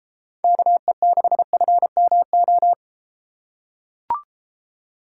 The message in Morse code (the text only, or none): KE6FMO